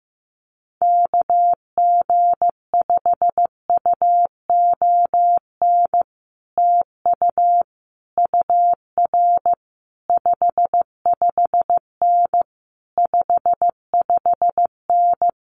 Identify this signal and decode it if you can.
Morse code, heard as a keyed tone, about 15 wpm, 700 Hz. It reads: KG5UON TU UR 55N 55N
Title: Common QSO Phrases Encore 15wpm